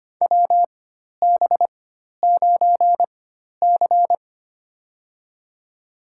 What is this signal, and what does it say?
Morse code, heard as a keyed tone, about 25 wpm, spaced as for 12 wpm, 700 Hz. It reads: WB9C
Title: USA Call Signs - 4 Characters - 4x Character Spacing - 25wpm